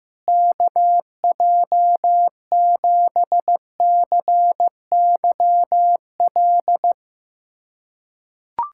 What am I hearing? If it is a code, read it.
Morse code, heard as a keyed tone, about 15 wpm, 700 Hz. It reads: KJ7CYL
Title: USA Callsign DE Callsign - 6 Characters - 15wpm